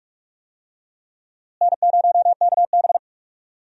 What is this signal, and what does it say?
Morse code, heard as a keyed tone, about 46 wpm, 700 Hz. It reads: N0KB
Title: USA Call Signs - 4 Characters - Repeated 3x - 45wpm